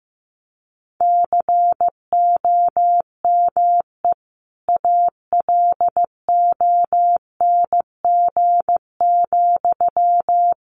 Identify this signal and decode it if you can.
Morse code, heard as a keyed tone, about 14 wpm, 700 Hz. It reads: COME ALONG,
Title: Sentences from Top 1500 Words 15wpm